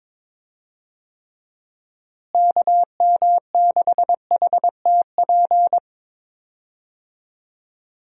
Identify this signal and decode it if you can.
Morse code, heard as a keyed tone, about 21 wpm, 700 Hz. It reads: KM6HTP
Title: USA Call Signs - 6 Characters - Repeated 2x - 22wpm